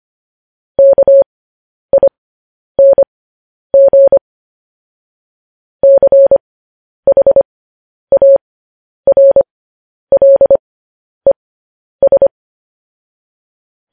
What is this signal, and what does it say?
Morse code, heard as a keyed tone, about 25 wpm, spaced as for 10 wpm, 550 Hz. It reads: KING CHARLES
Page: news headlines in morse code at 10 wpm